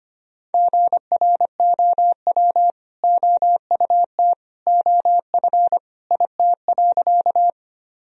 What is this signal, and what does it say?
Morse code, heard as a keyed tone, about 25 wpm, 700 Hz. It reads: GROW OUT OF IT.
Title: Sentences from Top 1000 Words - Review Part 1 - 25wpm